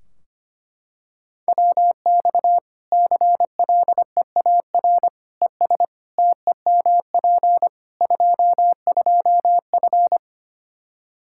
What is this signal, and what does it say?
Morse code, heard as a keyed tone, about 25 wpm, 700 Hz. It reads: WX CLEAR ES TEMP 22F
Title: Practice QSO - Exchange 2 - Info - 25wpm